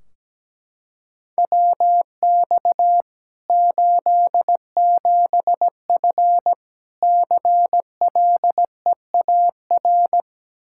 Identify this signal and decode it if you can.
Morse code, heard as a keyed tone, about 17 wpm, 700 Hz. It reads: WX 87F CLEAR
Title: Practice QSO - Exchange 2 - Info - 17wpm